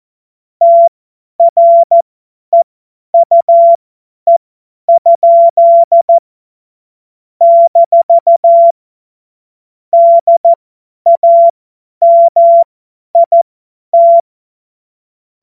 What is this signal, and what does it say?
Morse code, heard as a keyed tone, about 14 wpm, spaced as for 10 wpm, 680 Hz. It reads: TREUE? - DAMIT
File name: Morsekurs / Witze / Witz 01. November 2022 50 F70